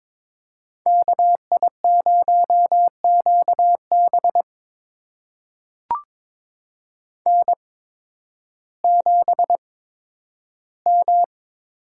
Morse code, heard as a keyed tone, about 21 wpm, spaced as for 12 wpm, 700 Hz. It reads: KI0QB N7M
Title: USA Call Signs - 5 Characters - 8x Character Spacing - 22wpm